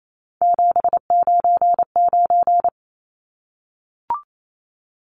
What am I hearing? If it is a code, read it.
Morse code, heard as a keyed tone, about 28 wpm, 700 Hz. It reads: 799
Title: Numbers - 3-Digit 28wpm